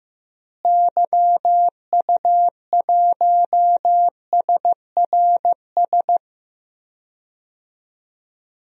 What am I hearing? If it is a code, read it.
Morse code, heard as a keyed tone, about 15 wpm, 700 Hz. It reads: YU1SRS
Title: DXCC-100 Call Signs - 6 Characters - Repeated 2x - 15wpm